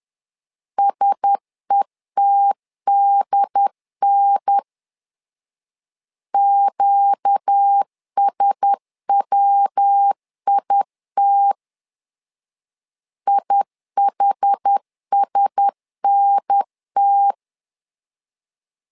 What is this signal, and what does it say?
Morse code, heard as a keyed tone, about 10 wpm, 800 Hz. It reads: SETDN QSWIT IHSNT
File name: LEZIONI / 4 Q1S 10wpm 2133